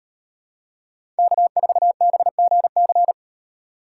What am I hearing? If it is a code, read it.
Morse code, heard as a keyed tone, about 35 wpm, 700 Hz. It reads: K4BGC